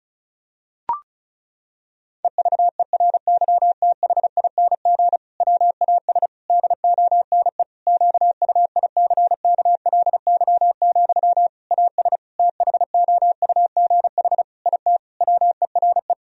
Morse code, heard as a keyed tone, about 34 wpm, 700 Hz. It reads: EVERYTHING WAS DONE QUICKLY, AS THOUGH IT WERE
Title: Sentences from Top 1000 Words - Review Part 2 - 35wpm